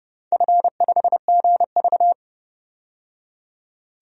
Morse code, heard as a keyed tone, about 30 wpm, 700 Hz. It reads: F5GV
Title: DXCC-100 Call Signs - 4 Characters - 30wpm